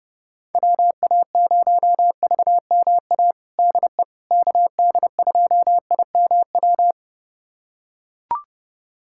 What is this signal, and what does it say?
Morse code, heard as a keyed tone, about 30 wpm, 700 Hz. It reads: WA0VMA DE KD2IMW
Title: USA Callsign DE Callsign 30wpm